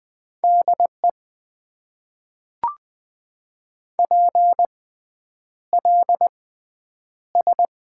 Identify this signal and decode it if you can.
Morse code, heard as a keyed tone, about 20 wpm, spaced as for 9 wpm, 700 Hz. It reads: DE PLS
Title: Top 90 QSO Elements - 6x Character Spacing - 20wpm